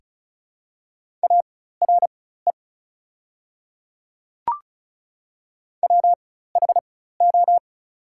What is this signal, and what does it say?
Morse code, heard as a keyed tone, about 36 wpm, spaced as for 16 wpm, 690 Hz. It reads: ARE WHO
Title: 4x Character Spacing - Top 300 Words - 35wpm